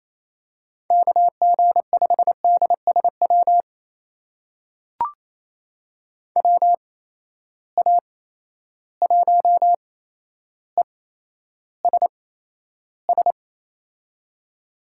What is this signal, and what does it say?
Morse code, heard as a keyed tone, about 28 wpm, spaced as for 12 wpm, 700 Hz. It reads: KG5DSW WA1ESS
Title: USA Call Signs - 6 Characters - 8x Character Spacing - 28wpm